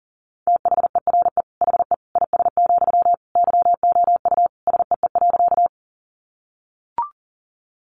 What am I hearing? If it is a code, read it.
Morse code, heard as a keyed tone, about 40 wpm, 700 Hz. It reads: THERE HE IS, YOU SEE.
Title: Sentences from Top 100 Words 40wpm